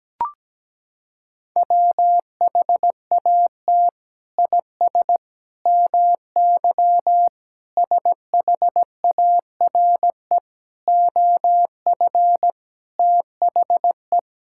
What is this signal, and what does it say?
Morse code, heard as a keyed tone, about 17 wpm, 700 Hz. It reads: WHAT IS MY SHARE OF THE
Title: Sentences from Top 1000 Words - Review Part 6 - 17wpm